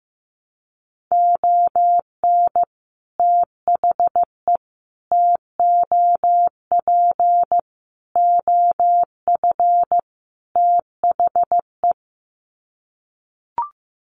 Morse code, heard as a keyed tone, about 15 wpm, 700 Hz. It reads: ON THE TOP OF THE
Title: Sets of 5 Words Encore 15wpm